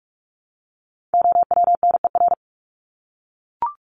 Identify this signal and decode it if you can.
Morse code, heard as a keyed tone, about 44 wpm, 700 Hz. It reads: OWNER
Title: Top 1501-1750 Words 45wpm